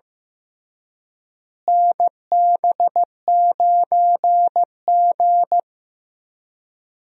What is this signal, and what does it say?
Morse code, heard as a keyed tone, about 15 wpm, 700 Hz. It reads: NB9G
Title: USA Call Signs - 4 Characters - Repeated 3x - 15wpm